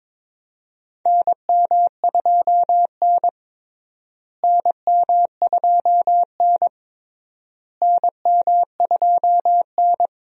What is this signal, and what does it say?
Morse code, heard as a keyed tone, about 22 wpm, 700 Hz. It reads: NM2N NM2N NM2N
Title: USA Call Signs - 4 Characters - Repeated 3x - 22wpm